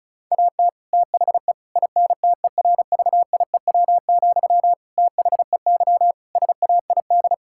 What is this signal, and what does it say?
Morse code, heard as a keyed tone, about 36 wpm, 700 Hz. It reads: AT THE INTERVIEW, THEY SAID
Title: Sentences from Top 1600 Words 35wpm